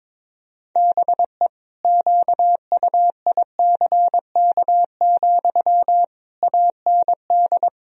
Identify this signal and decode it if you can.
Morse code, heard as a keyed tone, about 22 wpm, 700 Hz. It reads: BE QUICK, AND
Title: Sentences from Top 1000 Words - Review Part 3 - 22wpm